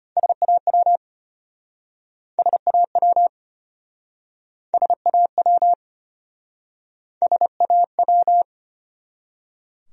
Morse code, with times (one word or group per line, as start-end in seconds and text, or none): 0.2-1.0 SAW
2.4-3.3 SAW
4.7-5.7 SAW
7.2-8.4 SAW